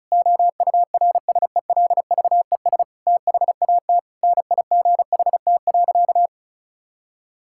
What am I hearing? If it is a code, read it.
Morse code, heard as a keyed tone, about 36 wpm, 700 Hz. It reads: OURSELVES THAT NIGHT.
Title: Sentences from Top 1600 Words 35wpm